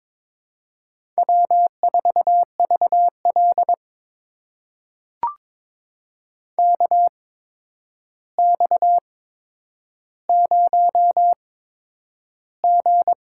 Morse code, heard as a keyed tone, about 22 wpm, spaced as for 9 wpm, 700 Hz. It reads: W4VL KX0G
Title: USA Call Signs - 4 Characters - 8x Character Spacing - 22wpm